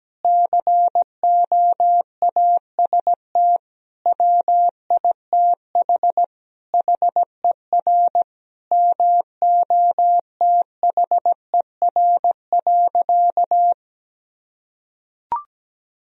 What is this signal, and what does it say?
Morse code, heard as a keyed tone, about 17 wpm, 700 Hz. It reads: COAST WITH HER MOTHER.
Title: Sentences from Top 1000 Words - Review Part 4 - 17wpm